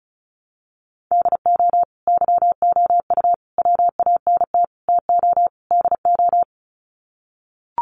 Morse code, heard as a keyed tone, about 34 wpm, 700 Hz. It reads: DO YOU WANT TO DO
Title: Sets of 5 Words Encore 35wpm